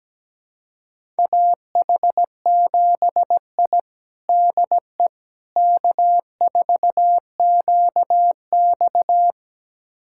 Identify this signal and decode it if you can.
Morse code, heard as a keyed tone, about 17 wpm, 700 Hz. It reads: AH7I DE K4QX
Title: USA Callsign DE Callsign - 4 Characters - 17wpm